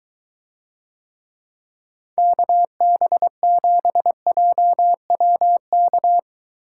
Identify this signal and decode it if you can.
Morse code, heard as a keyed tone, about 23 wpm, 700 Hz. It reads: KB7JWK